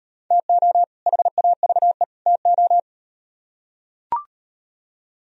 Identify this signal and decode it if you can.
Morse code, heard as a keyed tone, about 38 wpm, 700 Hz. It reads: TO HAVE TO